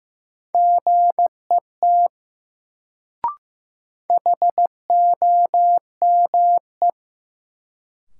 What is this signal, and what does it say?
Morse code, heard as a keyed tone, about 15 wpm, 700 Hz. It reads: GET HOME
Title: Sentences from Top 200 Words - New Word Drill - 15wpm